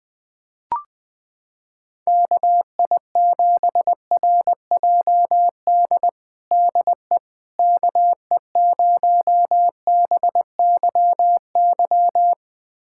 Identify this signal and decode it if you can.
Morse code, heard as a keyed tone, about 21 wpm, 700 Hz. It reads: KI7RJD DE KE0BYY
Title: USA Callsign DE Callsign - 6 Characters - 20wpm